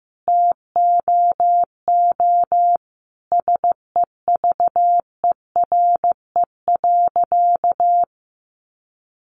Morse code, heard as a keyed tone, about 15 wpm, 700 Hz. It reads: TOO SEVERE.
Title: Sentences from Top 1400 Words 15wpm